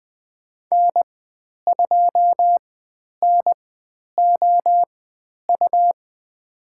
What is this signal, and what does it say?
Morse code, heard as a keyed tone, about 20 wpm, spaced as for 10 wpm, 700 Hz. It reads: N2NOU
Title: 109 Lesson 18g - Letter m - Call Signs 20wpm